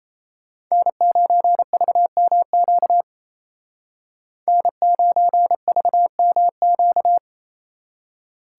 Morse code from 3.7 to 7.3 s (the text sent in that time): N9VMQ